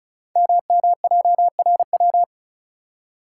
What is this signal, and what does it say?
Morse code, heard as a keyed tone, about 36 wpm, 690 Hz. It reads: MMJRW
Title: Code Groups - 5 Letters - 35wpm